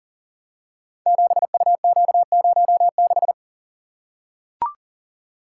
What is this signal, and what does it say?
Morse code, heard as a keyed tone, about 38 wpm, 690 Hz. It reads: 7UQ06
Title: Code Groups - 5 Letter:Numbers - 40wpm